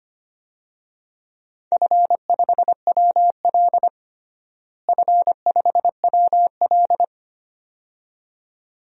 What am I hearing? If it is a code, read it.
Morse code, heard as a keyed tone, about 25 wpm, 700 Hz. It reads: F5WL F5WL
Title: DXCC-100 Call Signs - 4 Characters - Repeated 2x - 25wpm